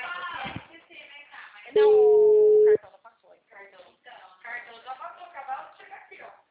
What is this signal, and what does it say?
Ringback tone (Europe, 425 Hz), with speech.
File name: out- -2004- - - .50999